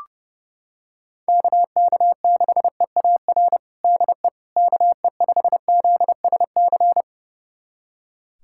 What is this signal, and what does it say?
Morse code, heard as a keyed tone, about 30 wpm, 700 Hz. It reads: KK6EAR DE KE5ZSC
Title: USA Callsign DE Callsign 30wpm